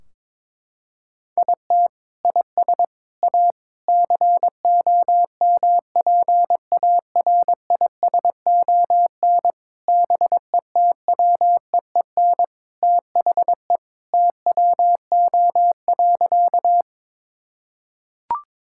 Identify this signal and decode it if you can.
Morse code, heard as a keyed tone, about 21 wpm, 700 Hz. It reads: IT IS A COMPARISON BETWEEN THE TWO.
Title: Sentences from Top 1600 Words 22wpm